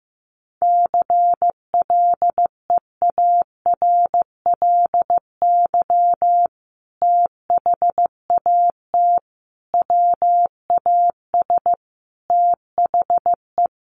Morse code, heard as a keyed tone, about 15 wpm, 700 Hz. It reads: CLEARLY THAT WAS THE
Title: Sentences from Top 1300 Words 15wpm